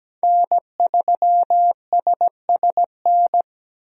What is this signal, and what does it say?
Morse code, heard as a keyed tone, about 17 wpm, 700 Hz. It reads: N3SSN
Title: USA Call Signs - 5 Characters - Repeated 3x - 17wpm